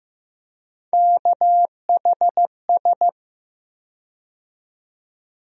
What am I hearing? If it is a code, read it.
Morse code, heard as a keyed tone, about 15 wpm, 700 Hz. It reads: KHS
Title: USA Call Sign Postfixes - 3 Letters 15wpm